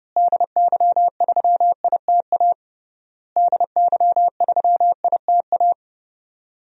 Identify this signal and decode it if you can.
Morse code, heard as a keyed tone, about 30 wpm, 700 Hz. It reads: DY3ITA DY3ITA
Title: DXCC-100 Call Signs - 6 Characters - Repeated 3x - 30wpm